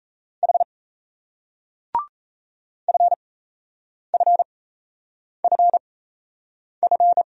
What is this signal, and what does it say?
Morse code, heard as a keyed tone, about 38 wpm, spaced as for 8 wpm, 700 Hz. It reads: H FFFF